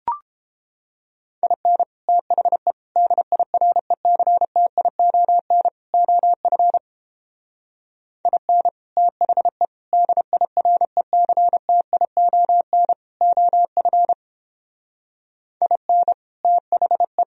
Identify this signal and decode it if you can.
Morse code, heard as a keyed tone, about 29 wpm, 700 Hz. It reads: IN THE DIRECTION OF IN THE DIRECTION OF IN THE